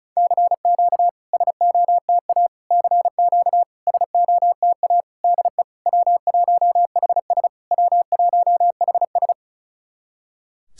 Morse code, heard as a keyed tone, about 36 wpm, 700 Hz. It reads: CQ SOTA CQ SOTA DE W1HS W1HS
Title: SOTA QSO Practice 35wpm